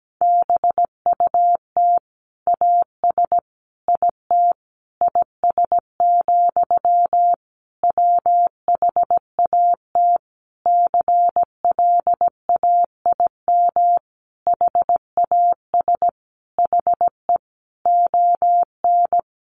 Morse code, heard as a keyed tone, about 17 wpm, 700 Hz. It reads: BUT AS IT IS, WHAT CLAIM HAS HE ON
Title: Sentences from Top 1500 Words 17wpm